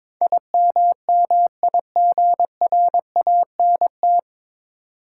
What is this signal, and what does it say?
Morse code, heard as a keyed tone, about 22 wpm, 700 Hz. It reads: IMMIGRANT
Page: Morse Code Ninja practice 22wpm